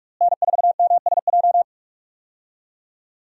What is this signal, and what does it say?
Morse code, heard as a keyed tone, about 47 wpm, 700 Hz. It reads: N4MSJ
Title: USA Call Signs - 5 Characters - Repeated 2x - 45wpm